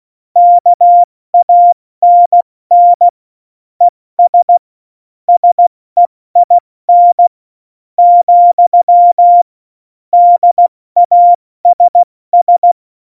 Morse code, heard as a keyed tone, about 16 wpm, 700 Hz. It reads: KANN ES SEIN , DASS
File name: Morsekurs / Witze / Witz 13 March 2018, 70 F80